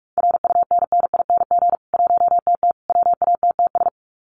Morse code, heard as a keyed tone, about 44 wpm, 700 Hz. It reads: RUNNING 1TT WATTS
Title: Common QSO Phrases 45wpm